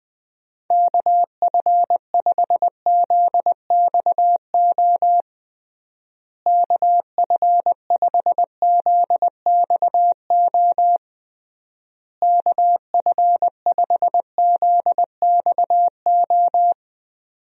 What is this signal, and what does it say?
Morse code, heard as a keyed tone, about 20 wpm, 700 Hz. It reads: KF5ZXO KF5ZXO KF5ZXO
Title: USA Call Signs - 6 Characters - Repeated 3x - 20wpm